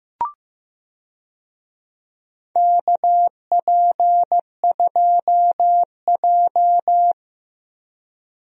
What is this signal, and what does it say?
Morse code, heard as a keyed tone, about 15 wpm, 700 Hz. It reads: KP2J
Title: DXCC-100 Call Signs - 4 Characters - Repeated 2x - 15wpm